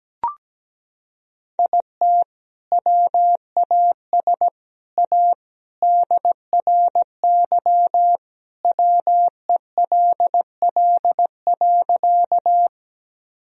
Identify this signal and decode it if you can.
Morse code, heard as a keyed tone, about 17 wpm, 700 Hz. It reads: IT WAS A DRY WELL.
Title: Sentences from Top 1000 Words - Review Part 3 - 17wpm